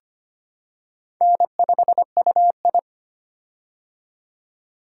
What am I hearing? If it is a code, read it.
Morse code, heard as a keyed tone, about 25 wpm, 700 Hz. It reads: N5UI
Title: K1USN SST QSO Practice - 25wpm